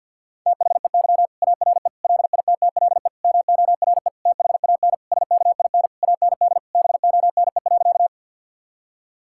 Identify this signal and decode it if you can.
Morse code, heard as a keyed tone, about 50 wpm, 700 Hz. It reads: THEY ARE LITTLE MORE THAN SKIN AND BONE.
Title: Sentences from Top 1600 Words 50wpm